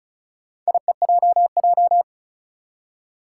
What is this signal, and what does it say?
Morse code, heard as a keyed tone, about 36 wpm, 700 Hz. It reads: IEJJ
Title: Four Challenging Letter-Number Combinations 35wpm